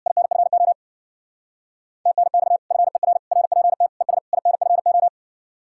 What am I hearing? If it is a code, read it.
Morse code, heard as a keyed tone, about 64 wpm, 700 Hz. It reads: INFO TNX FER RPT ES INFO